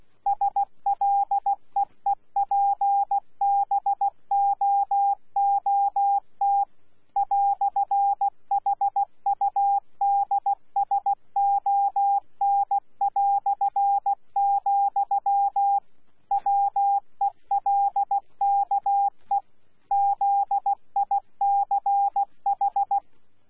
Morse code, heard as a keyed tone, about 16 wpm, 800 Hz. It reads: SLEEPBOOT "HUDSON", WELKE ZICH
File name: Z PI4VRZ / PI4VRZ 14-jun-2025